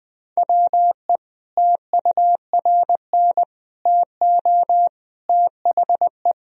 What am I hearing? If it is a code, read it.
Morse code, heard as a keyed tone, about 20 wpm, 700 Hz. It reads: WE TURN TO THE
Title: Sentences from Top 1000 Words - Review Part 1 - 20wpm